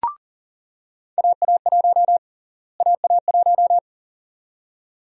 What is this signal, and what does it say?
Morse code, heard as a keyed tone, about 37 wpm, 700 Hz. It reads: AA1 AA1
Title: USA Call Sign Prefixes - Repeated 2x - 40wpm